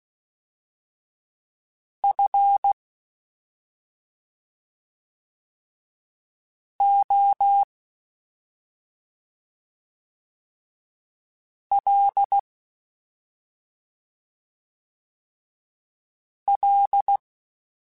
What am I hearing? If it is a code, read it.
Morse code, heard as a keyed tone, about 16 wpm, spaced as for 2 wpm, 800 Hz. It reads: FOLL
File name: Special INGs-mp3s / Special INGs-mp3s / Slow ing @10